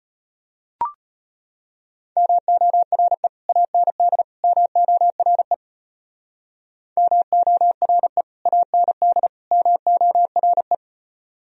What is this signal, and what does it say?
Morse code, heard as a keyed tone, about 30 wpm, 700 Hz. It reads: MORE AND MORE MORE AND MORE
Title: Speed-Racing - Sets of 3 Words 25wpm